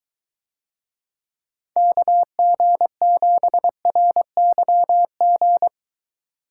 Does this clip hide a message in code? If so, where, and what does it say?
Yes: Morse code from 1.8 to 5.7 s, heard as a keyed tone, about 23 wpm, 700 Hz. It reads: KG7RYG